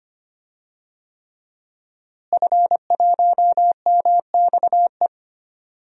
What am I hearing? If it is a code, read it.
Morse code, heard as a keyed tone, about 25 wpm, 700 Hz. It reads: F1MXE
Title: DXCC-100 Call Signs - 5 Characters - Repeated 3x - 25wpm